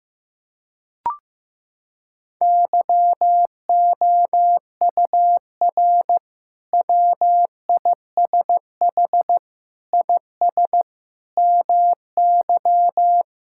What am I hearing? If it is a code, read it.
Morse code, heard as a keyed tone, about 16 wpm, 700 Hz. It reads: YOUR WISH IS MY
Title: Sentences from Top 1600 Words 15wpm